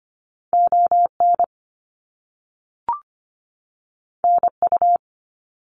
Morse code, heard as a keyed tone, about 25 wpm, 700 Hz. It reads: ON NU
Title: Canadian Provinces and Territories 25wpm